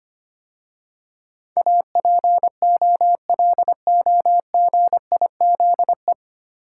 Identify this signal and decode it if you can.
Morse code, heard as a keyed tone, about 25 wpm, 700 Hz. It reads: APOLOGIZE
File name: practice / 25wpm / 9 Letter Words - First 3 Letters Given - 25wpm